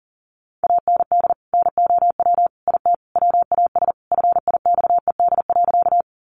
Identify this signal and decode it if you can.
Morse code, heard as a keyed tone, about 40 wpm, 700 Hz. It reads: AND NOW IT WAS FIXED.
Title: Sentences from Top 1100 Words 40wpm